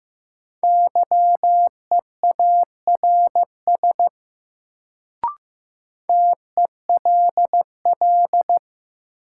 Morse code, heard as a keyed tone, about 15 wpm, 700 Hz. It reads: YEARS TELL
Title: Sentences from Top 200 Words - New Word Drill - 15wpm